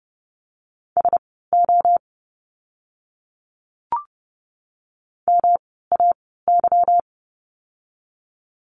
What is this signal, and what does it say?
Morse code, heard as a keyed tone, about 30 wpm, spaced as for 17 wpm, 700 Hz. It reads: SO MAY
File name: course / 30:10wpm / 230 Lesson 37f - Letter x - Words 30wpm